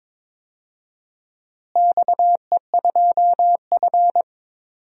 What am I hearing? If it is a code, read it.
Morse code, heard as a keyed tone, about 23 wpm, 700 Hz. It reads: XE2F
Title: DXCC-100 Call Signs - 4 Characters - Repeated 3x - 22wpm